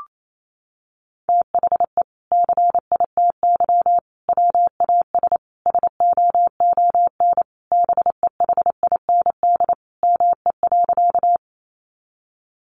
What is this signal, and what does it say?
Morse code, heard as a keyed tone, about 28 wpm, 700 Hz. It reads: THE CITY WAS SOON BEHIND ME.
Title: Sentences from Top 500 Words 28wpm